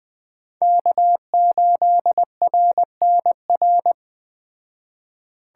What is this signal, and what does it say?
Morse code, heard as a keyed tone, about 20 wpm, 700 Hz. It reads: K8RNR